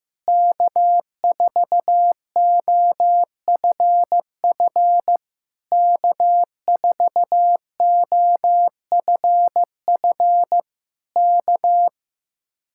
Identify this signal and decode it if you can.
Morse code, heard as a keyed tone, about 15 wpm, 700 Hz. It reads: K4OFF K4OFF K
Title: Speed-Racing Common QSO Phrases Encore - 15wpm